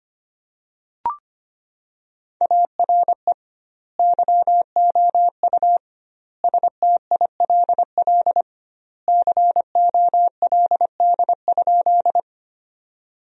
Morse code, heard as a keyed tone, about 25 wpm, 700 Hz. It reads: ARE YOU STILL COLD?
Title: 2x Word-Spacing - Sentences from Top 400 Words - 25wpm